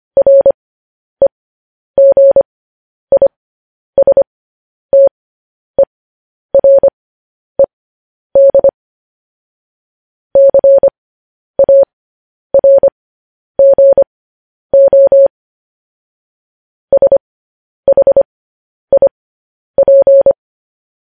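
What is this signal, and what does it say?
Morse code, heard as a keyed tone, about 25 wpm, spaced as for 10 wpm, 550 Hz. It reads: REGISTERED CARGO SHIP
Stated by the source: Morse code news headlines.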